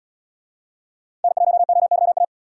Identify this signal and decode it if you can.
Morse code, heard as a keyed tone, about 69 wpm, 700 Hz. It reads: N1OYA